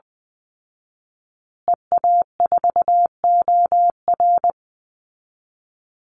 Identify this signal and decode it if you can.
Morse code, heard as a keyed tone, about 20 wpm, 700 Hz. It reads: EA4OR
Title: Call Signs Repeated 2x - 20wpm